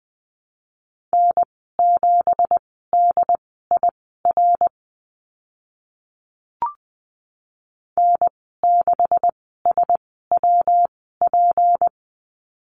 Morse code, heard as a keyed tone, about 20 wpm, spaced as for 15 wpm, 700 Hz. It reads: N7DIR N6SWP
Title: 193 Lesson 31g - Letter v - Call Signs 20wpm